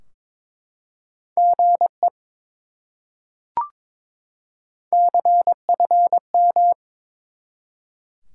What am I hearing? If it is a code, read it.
Morse code, heard as a keyed tone, about 22 wpm, 700 Hz. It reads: GE CFM
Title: Top 180 QSO Elements 22wpm